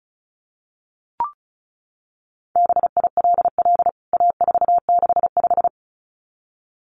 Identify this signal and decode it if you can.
Morse code, heard as a keyed tone, about 34 wpm, 700 Hz. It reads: BILL A465
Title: CWT Exchange 35wpm